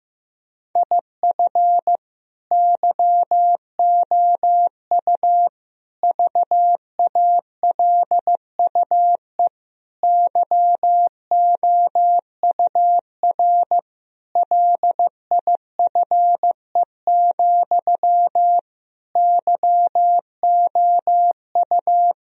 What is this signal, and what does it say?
Morse code, heard as a keyed tone, about 15 wpm, 700 Hz. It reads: IF YOU VALUE YOUR LIFE, YOU
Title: Sentences from Top 1000 Words - Review Part 3 - 15wpm